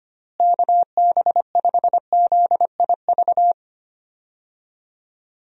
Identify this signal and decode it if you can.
Morse code, heard as a keyed tone, about 25 wpm, 700 Hz. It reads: KB5ZIV
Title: USA Call Signs - 6 Characters - Repeated 3x - 25wpm